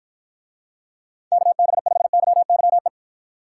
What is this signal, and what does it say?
Morse code, heard as a keyed tone, about 53 wpm, 700 Hz. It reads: KB5YYE